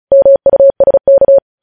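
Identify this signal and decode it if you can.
Morse code, heard as a keyed tone, about 36 wpm, 550 Hz. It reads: MUSK
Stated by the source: Morse code news headlines at 35.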